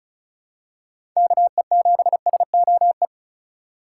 Morse code, heard as a keyed tone, about 36 wpm, 700 Hz. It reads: KE7SOE
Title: USA Call Signs - 6 Characters - 2x Character Spacing - 35wpm